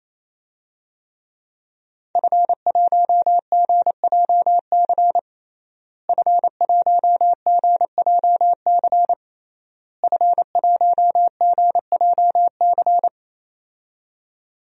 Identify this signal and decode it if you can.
Morse code, heard as a keyed tone, about 28 wpm, 700 Hz. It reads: F1GJC F1GJC F1GJC
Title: DXCC-100 Call Signs - 5 Characters - Repeated 3x - 28wpm